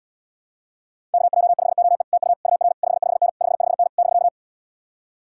Morse code, heard as a keyed tone, about 79 wpm, 700 Hz. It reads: KJ5OE TU UR 55N 55N <BK>